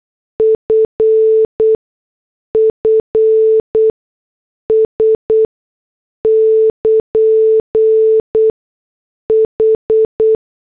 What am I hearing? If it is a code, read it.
Morse code, heard as a keyed tone, about 8 wpm, spaced as for 6 wpm, 440 Hz. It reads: FFS <KN> H
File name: morse / headlines